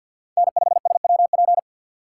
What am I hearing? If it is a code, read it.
Morse code, heard as a keyed tone, about 61 wpm, 700 Hz. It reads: N5SWP
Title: Speed Racing - USA 1x3 Call Signs 50wpm